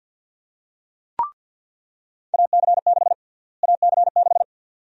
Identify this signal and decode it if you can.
Morse code, heard as a keyed tone, about 47 wpm, 700 Hz. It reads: AC6 AC6
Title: USA Call Sign Prefixes - Repeated 2x - 50wpm